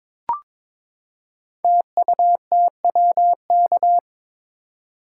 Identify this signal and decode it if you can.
Morse code, heard as a keyed tone, about 23 wpm, 690 Hz. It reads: TUTWK
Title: Code Groups - 5 Letters - 22wpm